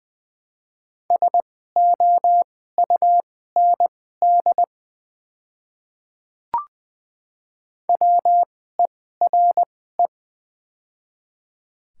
Morse code, heard as a keyed tone, about 20 wpm, spaced as for 14 wpm, 700 Hz. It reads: SOUND WERE
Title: 2x Character Spacing - Top 100 Words - 20wpm